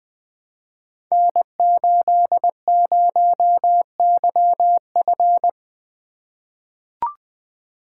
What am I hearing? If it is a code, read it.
Morse code, heard as a keyed tone, about 20 wpm, 700 Hz. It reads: N80YF
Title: Code Groups - 5 Letter:Numbers - 20wpm